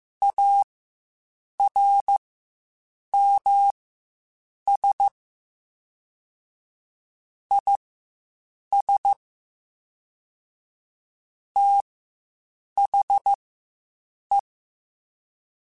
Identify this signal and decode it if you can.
Morse code, heard as a keyed tone, about 15 wpm, spaced as for 7 wpm, 800 Hz. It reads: ARMS IS THE
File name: Quotes Spaced / Quotes Spaced-mp3s / ChildAsleep sp @10